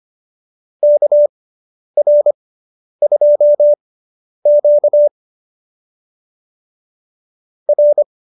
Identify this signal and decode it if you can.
Morse code, heard as a keyed tone, about 25 wpm, spaced as for 10 wpm, 590 Hz. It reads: KR2Q R